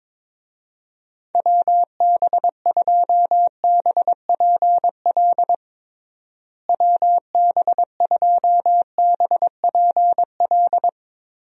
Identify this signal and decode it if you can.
Morse code, heard as a keyed tone, about 22 wpm, 700 Hz. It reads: WB2BPL WB2BPL
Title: USA Call Signs - 6 Characters - Repeated 3x - 22wpm